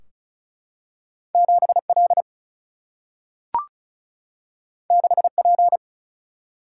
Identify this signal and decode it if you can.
Morse code, heard as a keyed tone, about 36 wpm, 700 Hz. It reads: 7 L 6 P
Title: Single-Letter Single-Number 35wpm